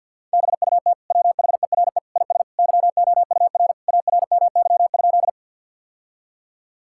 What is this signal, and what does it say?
Morse code, heard as a keyed tone, about 52 wpm, 700 Hz. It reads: BUT WHERE IS YOUR ARMY?
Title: Sentences from Top 1000 Words - Review Part 5 - 50wpm